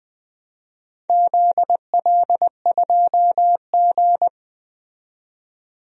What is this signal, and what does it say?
Morse code, heard as a keyed tone, about 20 wpm, 700 Hz. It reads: ZL2G
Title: DXCC-100 Call Signs - 4 Characters - 20wpm